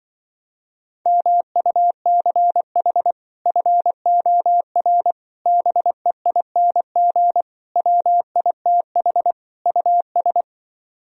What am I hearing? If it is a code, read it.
Morse code, heard as a keyed tone, about 25 wpm, 700 Hz. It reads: MUCH FOR BEING WITH US